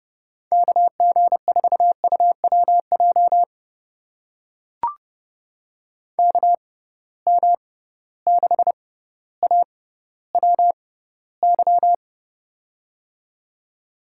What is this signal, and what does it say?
Morse code, heard as a keyed tone, about 30 wpm, spaced as for 15 wpm, 700 Hz. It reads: KG4UWJ KM6AWY
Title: USA Call Signs - 6 Characters - 6x Character Spacing - 30wpm